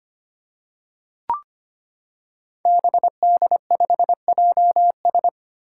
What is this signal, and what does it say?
Morse code, heard as a keyed tone, about 25 wpm, 700 Hz. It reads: BD5JS
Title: Five Challenging Letter-Number Combinations 25wpm